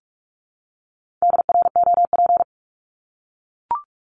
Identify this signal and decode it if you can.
Morse code, heard as a keyed tone, about 44 wpm, 700 Hz. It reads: DROP
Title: Top 1251-1500 Words 45wpm